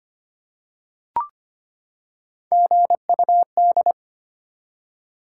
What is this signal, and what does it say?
Morse code, heard as a keyed tone, about 25 wpm, 700 Hz. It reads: GUD
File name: practice / 25wpm / Top 20 QSO Elements - 25wpm